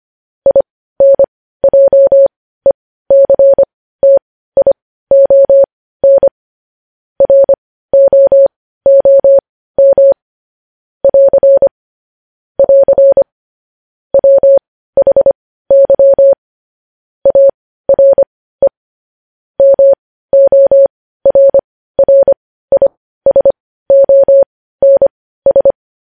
Morse code, heard as a keyed tone, about 25 wpm, spaced as for 15 wpm, 550 Hz. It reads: INJECTION ROOM <AR> <AR> WHY ARE MORRISONS
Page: news-headlines-in-morse-code-at-15-wpm